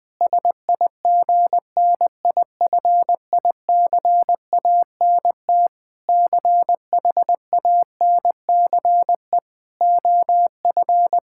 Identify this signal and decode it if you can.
Morse code, heard as a keyed tone, about 19 wpm, 700 Hz. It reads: SIGNIFICANT CHANCE OF
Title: Sentences from Top 1000 Words - Review Part 6 - 20wpm